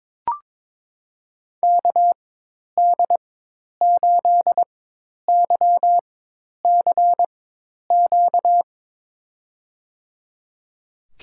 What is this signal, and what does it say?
Morse code, heard as a keyed tone, about 22 wpm, spaced as for 10 wpm, 700 Hz. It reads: KD8YCQ
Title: USA Call Signs - 6 Characters - 4x Character Spacing - 22wpm